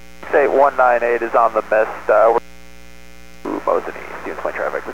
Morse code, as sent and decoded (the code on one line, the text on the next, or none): none